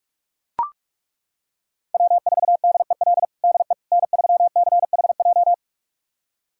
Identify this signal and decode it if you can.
Morse code, heard as a keyed tone, about 42 wpm, 700 Hz. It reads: W4DEL DE N3CHJ
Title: USA Callsign DE Callsign - 5 Characters - 45wpm